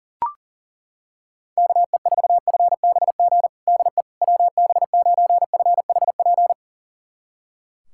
Morse code, heard as a keyed tone, about 40 wpm, 700 Hz. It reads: KE4FBG DE WB9FHP
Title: USA Callsign DE Callsign - 6 Characters - 40wpm